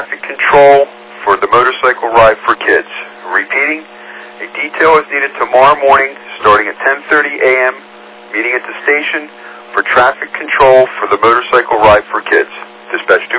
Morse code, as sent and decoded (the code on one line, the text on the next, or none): none